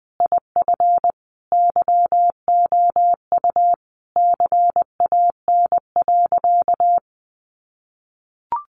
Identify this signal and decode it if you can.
Morse code, heard as a keyed tone, about 20 wpm, 700 Hz. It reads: IF YOU CAN.
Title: Sentences from Top 1100 Words 20wpm